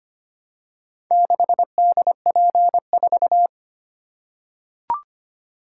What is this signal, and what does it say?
Morse code, heard as a keyed tone, about 25 wpm, 700 Hz. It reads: 6DP4
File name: practice / 25wpm / Four Letter-Number Combinations - 25wpm